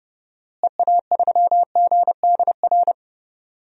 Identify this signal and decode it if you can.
Morse code, heard as a keyed tone, about 30 wpm, 700 Hz. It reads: EA3GDR
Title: DXCC-100 Call Signs - 6 Characters - Repeated 2x - 30wpm